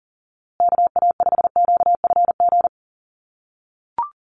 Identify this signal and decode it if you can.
Morse code, heard as a keyed tone, about 38 wpm, 700 Hz. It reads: KA5QFG
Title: USA Call Signs Repeated 3x 40wpm